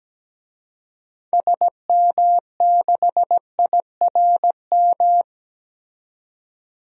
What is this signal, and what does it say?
Morse code, heard as a keyed tone, about 17 wpm, 700 Hz. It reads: SM6IRM
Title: DXCC-100 Call Signs - 6 Characters - 17wpm